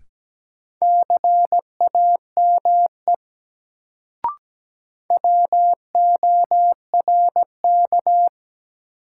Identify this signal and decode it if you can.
Morse code, heard as a keyed tone, about 17 wpm, 700 Hz. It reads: CAME WORK
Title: Sentences from Top 200 Words - New Word Drill - 17wpm